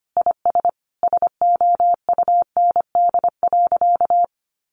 Morse code, heard as a keyed tone, about 25 wpm, 700 Hz. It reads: IS SOUND.
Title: Sentences from Top 1400 Words 25wpm